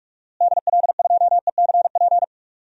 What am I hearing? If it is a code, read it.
Morse code, heard as a keyed tone, about 42 wpm, 700 Hz. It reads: DL2ECP
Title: DXCC-100 Call Signs - 6 Characters - Repeated 3x - 45wpm